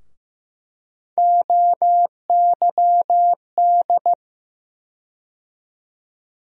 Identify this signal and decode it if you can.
Morse code, heard as a keyed tone, about 15 wpm, 700 Hz. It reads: OYD
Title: USA Call Sign Postfixes - 3 Letters 15wpm